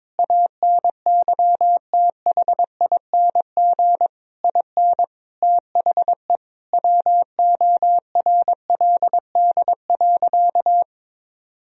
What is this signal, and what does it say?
Morse code, heard as a keyed tone, about 22 wpm, 700 Hz. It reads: ANYTHING IN THE WORLD.
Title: Sentences from Top 1000 Words - Review Part 3 - 22wpm